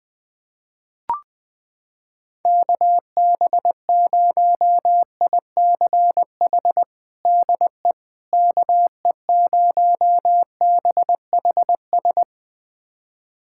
Morse code, heard as a keyed tone, about 20 wpm, 700 Hz. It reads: KB0ICH DE KE0BHS
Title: USA Callsign DE Callsign - 6 Characters - 20wpm